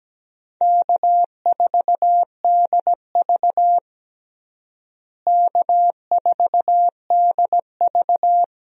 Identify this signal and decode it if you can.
Morse code, heard as a keyed tone, about 17 wpm, 700 Hz. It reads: K4DV K4DV
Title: USA Call Signs - 4 Characters - Repeated 2x - 17wpm